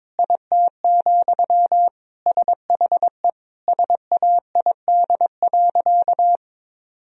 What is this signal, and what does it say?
Morse code, heard as a keyed tone, about 22 wpm, 700 Hz. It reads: IT, SHE SAID.
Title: Sentences from Top 1000 Words - Review Part 4 - 22wpm